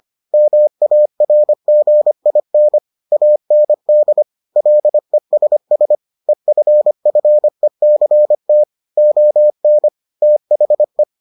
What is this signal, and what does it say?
Morse code, heard as a keyed tone, about 25 wpm, 600 Hz. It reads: MARGIN AND LESS EFFECT ON THE